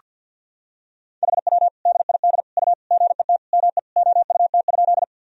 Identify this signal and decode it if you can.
Morse code, heard as a keyed tone, about 46 wpm, 700 Hz. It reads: HW DID U GET ME OUT?
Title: Sentences from Top 200 Words and Abbreviations 50wpm